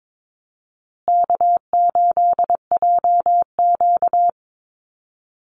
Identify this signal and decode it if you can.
Morse code, heard as a keyed tone, about 22 wpm, 700 Hz. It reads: K8JQ
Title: Easy Call Signs Repeated 3x - 22wpm